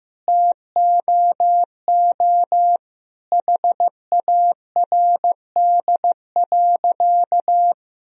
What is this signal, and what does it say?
Morse code, heard as a keyed tone, about 15 wpm, 700 Hz. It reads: TOO HARD.
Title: Sentences from Top 1000 Words - Review Part 1 - 15wpm